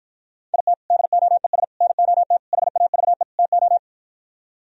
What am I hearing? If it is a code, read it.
Morse code, heard as a keyed tone, about 52 wpm, 700 Hz. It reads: IT DOES NOT HAVE TO